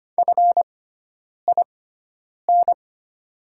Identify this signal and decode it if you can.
Morse code, heard as a keyed tone, about 25 wpm, spaced as for 9 wpm, 700 Hz. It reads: FIN
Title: 6x Character Spacing - Top 500 Words - 25wpm